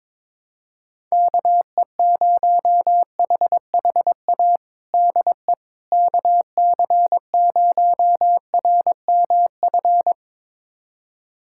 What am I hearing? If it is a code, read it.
Morse code, heard as a keyed tone, about 22 wpm, 700 Hz. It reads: KE0HHA DE KC0RMF
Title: USA Callsign DE Callsign 22wpm